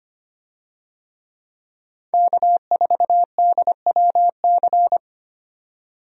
Morse code, heard as a keyed tone, about 25 wpm, 700 Hz. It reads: K4DWC
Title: USA Call Signs - 5 Characters - 25wpm